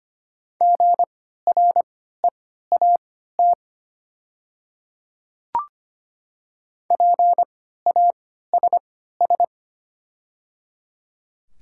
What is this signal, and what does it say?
Morse code, heard as a keyed tone, about 25 wpm, spaced as for 14 wpm, 700 Hz. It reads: GREAT PASS
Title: 3x Character Spacing - Top 500 Words - 25wpm